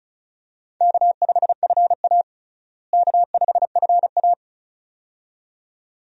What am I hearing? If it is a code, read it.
Morse code, heard as a keyed tone, about 34 wpm, 700 Hz. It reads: K5FA K5FA
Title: USA Call Signs - 4 Characters - Repeated 2x - 35wpm